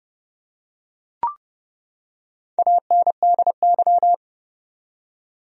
Morse code, heard as a keyed tone, about 30 wpm, 700 Hz. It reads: ANDY
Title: USA Common Names - Max 4 Letters - 30wpm